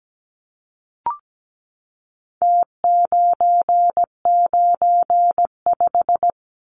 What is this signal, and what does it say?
Morse code, heard as a keyed tone, about 17 wpm, 700 Hz. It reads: T995
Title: Numbers - Contest Style - 4-Digit 17wpm